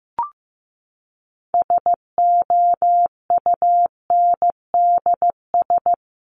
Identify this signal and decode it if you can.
Morse code, heard as a keyed tone, about 14 wpm, 700 Hz. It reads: SOUNDS
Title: A Sound Of Thunder 15wpm